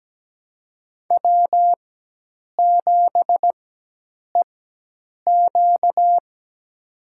Morse code, heard as a keyed tone, about 17 wpm, spaced as for 8 wpm, 700 Hz. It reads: W7EQ
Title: USA Call Signs - 4x Character Spacing - 17wpm